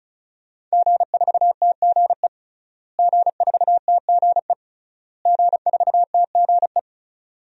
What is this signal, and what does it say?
Morse code, heard as a keyed tone, about 36 wpm, 700 Hz. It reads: G4TGE G4TGE G4TGE
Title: DXCC-100 Call Signs - 5 Characters - Repeated 3x - 35wpm